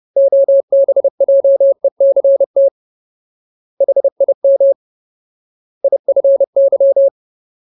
Morse code, heard as a keyed tone, about 30 wpm, 550 Hz. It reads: OBJECT HIM IFY
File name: Ultimate Word List / 30WPM Ultimate